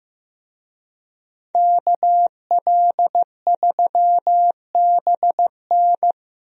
Morse code, heard as a keyed tone, about 15 wpm, 700 Hz. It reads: KL3BN
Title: USA Call Signs - 5 Characters - Repeated 3x - 15wpm